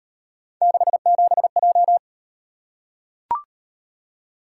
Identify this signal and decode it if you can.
Morse code, heard as a keyed tone, about 36 wpm, 700 Hz. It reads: 67J